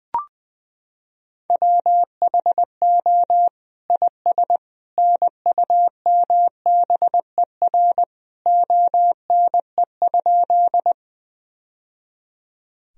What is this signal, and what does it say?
Morse code, heard as a keyed tone, about 21 wpm, 700 Hz. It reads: WHO IS NUMBER ONE?
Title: Sentences from Top 1000 Words - Review Part 1 - 20wpm